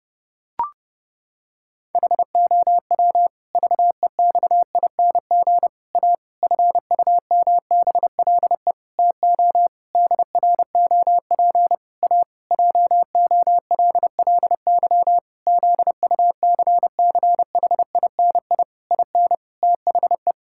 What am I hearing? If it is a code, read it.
Morse code, heard as a keyed tone, about 30 wpm, 700 Hz. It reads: HOW VEXING A FUMBLE TO DROP A JOLLY ZUCCHINI IN THE
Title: Pangram Challenge v2 - 30wpm